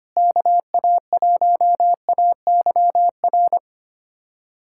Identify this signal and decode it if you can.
Morse code, heard as a keyed tone, about 25 wpm, 700 Hz. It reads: KA1AYR
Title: USA Callsign DE Callsign - 6 Characters - 25wpm